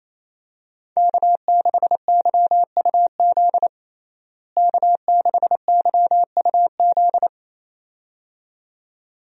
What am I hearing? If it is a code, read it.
Morse code, heard as a keyed tone, about 28 wpm, 700 Hz. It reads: K6YUZ K6YUZ
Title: USA Call Signs - 5 Characters - Repeated 2x - 28wpm